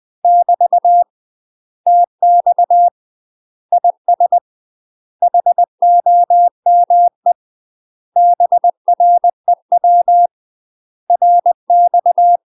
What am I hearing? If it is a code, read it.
Morse code, heard as a keyed tone, about 20 wpm, 700 Hz. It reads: <BT> TX IS HOME BREW RX
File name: UK MASTER QSOs TRAINING / 20WPM / UK MASTER QSOs 0065